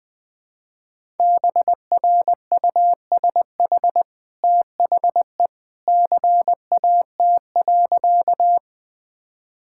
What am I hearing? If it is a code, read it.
Morse code, heard as a keyed tone, about 20 wpm, 700 Hz. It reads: BRUSH THE CAT.
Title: Easy Sentences with Hints - 20wpm